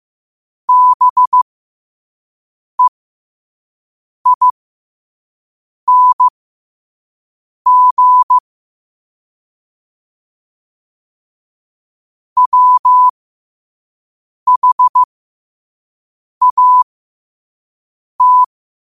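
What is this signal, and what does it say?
Morse code, heard as a keyed tone, about 15 wpm, spaced as for 5 wpm, 1000 Hz. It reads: BEING WHAT
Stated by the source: Morse code sent at 5 wpm.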